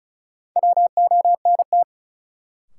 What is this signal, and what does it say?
Morse code, heard as a keyed tone, about 36 wpm, 700 Hz. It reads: WONT
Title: Sentences from Top 1000 Words - Review Part 6 - New Word Drill - 35wpm